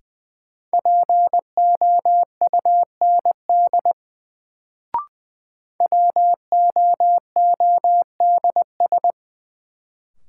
Morse code, heard as a keyed tone, about 20 wpm, 700 Hz. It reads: POUND WOODS
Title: Sentences from Top 1000 Words - Review Part 2 - New Word Drill - 20wpm